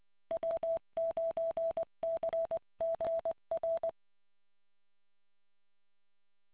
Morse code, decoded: W9CGR